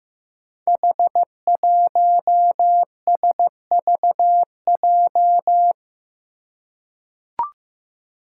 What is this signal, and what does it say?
Morse code, heard as a keyed tone, about 15 wpm, 700 Hz. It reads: H1SVJ
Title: Five Challenging Letter-Number Combinations 15wpm